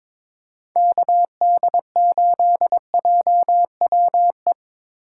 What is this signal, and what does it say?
Morse code, heard as a keyed tone, about 22 wpm, 700 Hz. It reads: KD8JWE
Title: USA Call Signs - 3x Character Spacing - 22wpm